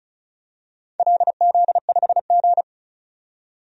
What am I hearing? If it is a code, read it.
Morse code, heard as a keyed tone, about 36 wpm, 700 Hz. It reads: LZ5G
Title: DXCC-100 Call Signs - 4 Characters - 35wpm